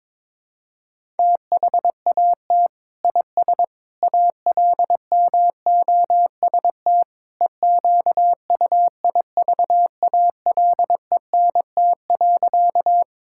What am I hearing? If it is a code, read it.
Morse code, heard as a keyed tone, about 22 wpm, 700 Hz. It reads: THAT IS ALMOST EQUIVALENT.
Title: Sentences from Top 1600 Words 22wpm